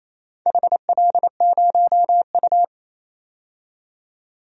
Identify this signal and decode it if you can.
Morse code, heard as a keyed tone, about 28 wpm, 700 Hz. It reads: HL0U
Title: DXCC-100 Call Signs - 4 Characters - 28wpm